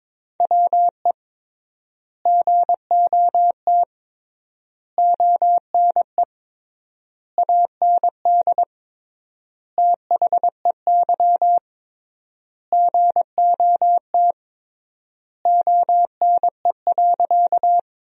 Morse code, heard as a keyed tone, about 22 wpm, 700 Hz. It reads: WE GOT ONE AND THEY GOT ONE.
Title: 3x Word-Spacing - Sentences from Top 300 Words - 22wpm